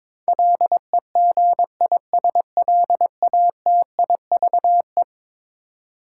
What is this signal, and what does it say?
Morse code, heard as a keyed tone, about 22 wpm, 700 Hz. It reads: LEGISLATIVE
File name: practice / 22wpm / Eleven Letter Words - 22wpm